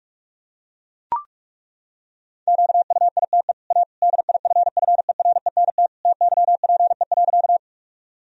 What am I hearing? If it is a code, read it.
Morse code, heard as a keyed tone, about 41 wpm, 700 Hz. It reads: QUITE A DIFFERENT TYPE.
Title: Sentences from Top 1000 Words - Review Part 3 - 45wpm